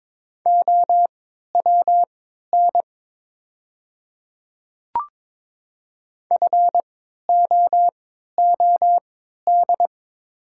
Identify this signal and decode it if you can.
Morse code, heard as a keyed tone, about 22 wpm, spaced as for 13 wpm, 700 Hz. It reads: OWN FOOD
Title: 3x Character Spacing - Top 500 Words - 22wpm